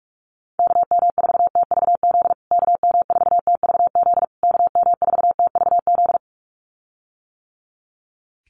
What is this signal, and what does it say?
Morse code, heard as a keyed tone, about 44 wpm, 700 Hz. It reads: KM4TVZ KM4TVZ KM4TVZ
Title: USA Call Signs Repeated 3x 45wpm